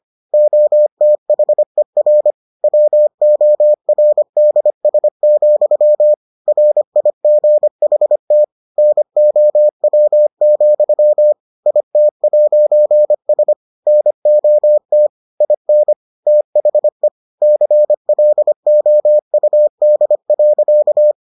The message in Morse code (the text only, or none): OTHER WORDS, RIGHT NOW, IT'S NOT IN THE CLOUD.